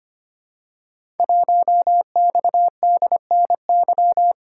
Morse code, heard as a keyed tone, about 25 wpm, 700 Hz. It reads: 1XDNY
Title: Code Groups - 5 Letter:Numbers - 25wpm